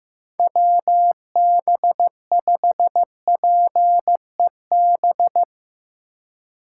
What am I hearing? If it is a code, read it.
Morse code, heard as a keyed tone, about 15 wpm, 700 Hz. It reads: WB5PEB
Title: USA Call Signs - 6 Characters - 3x Character Spacing - 15wpm